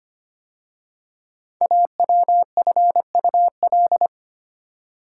Morse code, heard as a keyed tone, about 25 wpm, 700 Hz. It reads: AWFUL
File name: 5 Letter Words - First 3 Letters Given - 25wpm